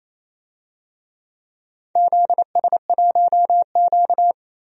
Morse code, heard as a keyed tone, about 28 wpm, 700 Hz. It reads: ZS1Q
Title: DXCC-100 Call Signs - 4 Characters - Repeated 3x - 28wpm